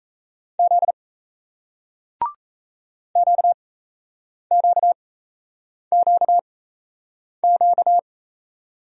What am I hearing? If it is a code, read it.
Morse code, heard as a keyed tone, about 39 wpm, spaced as for 8 wpm, 700 Hz. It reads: Z QQQQ